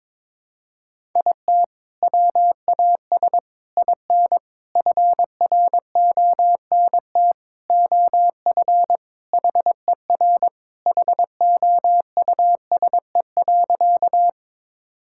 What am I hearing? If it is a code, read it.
Morse code, heard as a keyed tone, about 22 wpm, 700 Hz. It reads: IT WAS IN FRONT OF HER HOUSE.
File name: practice / 22wpm / Sentences with 4-Word Phrases - 22wpm